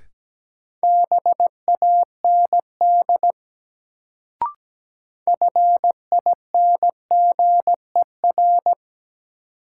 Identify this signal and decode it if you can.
Morse code, heard as a keyed tone, about 17 wpm, 700 Hz. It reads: BAND FINGER
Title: Sentences from Top 1600 Words - New Word Drill Alt - 17wpm